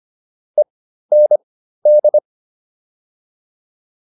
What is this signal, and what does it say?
Morse code, heard as a keyed tone, about 25 wpm, spaced as for 13 wpm, 600 Hz. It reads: END